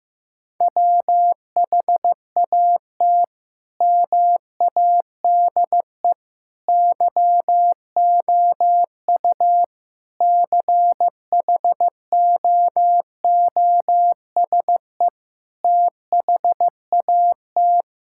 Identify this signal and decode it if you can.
Morse code, heard as a keyed tone, about 15 wpm, 700 Hz. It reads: WHAT MADE YOU CHOOSE THAT
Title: Sentences from Top 1000 Words - Review Part 5 - 15wpm